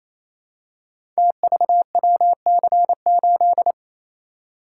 Morse code, heard as a keyed tone, about 29 wpm, 700 Hz. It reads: TVWC8
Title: Code Groups - 5 Letter:Numbers - 28wpm